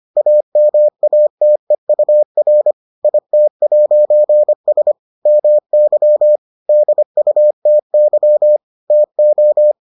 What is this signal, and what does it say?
Morse code, heard as a keyed tone, about 25 wpm, 600 Hz. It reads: AMATEUR IT'S MY DUTY TO